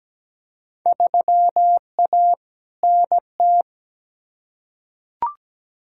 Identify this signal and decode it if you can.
Morse code, heard as a keyed tone, about 17 wpm, 700 Hz. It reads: 3A NT
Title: Field Day Exchanges Encore - 17wpm